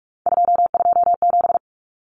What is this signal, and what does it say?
Morse code, heard as a keyed tone, about 44 wpm, 700 Hz. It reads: 227
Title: Common Cut Numbers - 3-Digit 45wpm